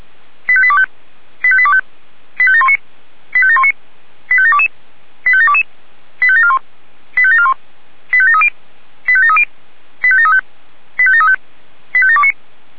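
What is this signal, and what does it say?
Five-tone selective call (ZVEI), heard as quick runs of five tones, tones of about 70 ms. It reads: 85825 85825 85719 85719 85722 85722 85741 85741 85730 85730 85735 85735 85729
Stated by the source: ZVEI five-tone call received by SDR, from audio.f85360.feuerwehr-orlamuende.org, a fire brigade (Feuerwehr Orlamünde) server.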